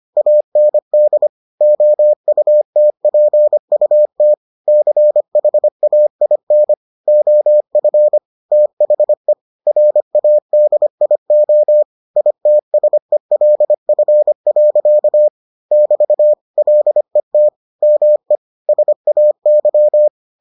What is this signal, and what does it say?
Morse code, heard as a keyed tone, about 25 wpm, 600 Hz. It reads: AND OUTPUT CHAIN OF THE RADIO ITSELF. <BT> LET ME SAY